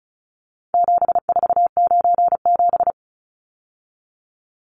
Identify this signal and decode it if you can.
Morse code, heard as a keyed tone, about 34 wpm, 700 Hz. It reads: 7497
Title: Numbers - 4-Digit 35wpm